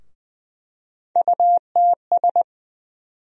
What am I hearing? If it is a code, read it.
Morse code, heard as a keyed tone, about 20 wpm, 700 Hz. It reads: UTS
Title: USA Call Sign Postfixes - 3 Letters 20wpm